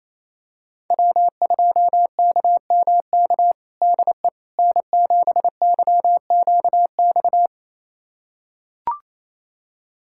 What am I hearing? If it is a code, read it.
Morse code, heard as a keyed tone, about 28 wpm, 700 Hz. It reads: W2KMK DE N7YQX
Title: USA Callsign DE Callsign - 5 Characters - 28wpm